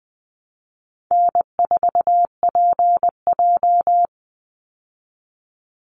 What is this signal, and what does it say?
Morse code, heard as a keyed tone, about 20 wpm, 700 Hz. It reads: N4PJ
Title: Call Signs Repeated 3x - 20wpm